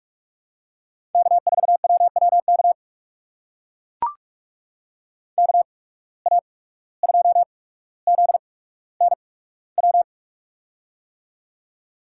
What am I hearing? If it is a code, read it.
Morse code, heard as a keyed tone, about 44 wpm, spaced as for 18 wpm, 700 Hz. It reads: K4WWK KA2ZNW
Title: USA Call Signs - 8x Character Spacing - 45wpm